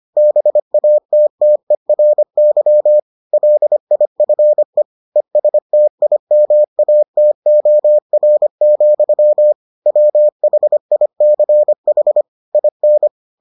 Morse code, heard as a keyed tone, about 25 wpm, 600 Hz. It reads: BATTERY LIFE ESTIMATOR, WHICH IN